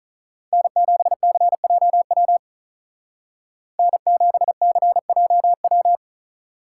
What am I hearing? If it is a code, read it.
Morse code, heard as a keyed tone, about 43 wpm, 700 Hz. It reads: N7CJW N7CJW
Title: Speed Racing - USA 1x3 Call Signs 35wpm